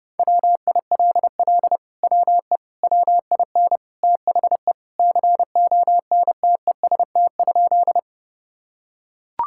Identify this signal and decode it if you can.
Morse code, heard as a keyed tone, about 30 wpm, 700 Hz. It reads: WILL WE WIN THE CONTEST?
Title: Sentences from Top 1600 Words 30wpm